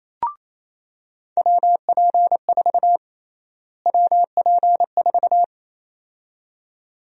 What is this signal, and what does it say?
Morse code, heard as a keyed tone, about 28 wpm, 700 Hz. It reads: WP4 WP4
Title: USA Call Sign Prefixes - Repeated 2x - 28wpm